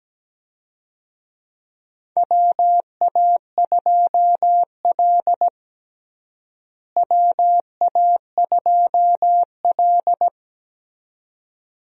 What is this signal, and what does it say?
Morse code, heard as a keyed tone, about 17 wpm, 700 Hz. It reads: WA2L WA2L
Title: USA Call Signs - 4 Characters - Repeated 2x - 17wpm